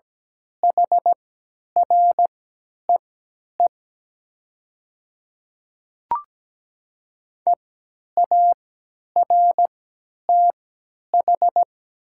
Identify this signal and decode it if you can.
Morse code, heard as a keyed tone, about 17 wpm, spaced as for 10 wpm, 700 Hz. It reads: HREE EARTH
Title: 3x Character Spacing - Top 200 Words - 17wpm